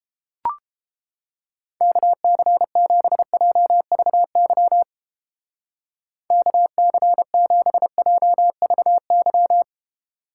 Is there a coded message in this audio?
Yes: Morse code from 1.8 to 9.6 s, heard as a keyed tone, about 29 wpm, 700 Hz. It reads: KC7JVY KC7JVY